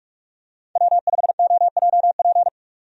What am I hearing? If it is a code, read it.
Morse code, heard as a keyed tone, about 44 wpm, 700 Hz. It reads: W5OJP
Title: USA Call Signs - 5 Characters - 45wpm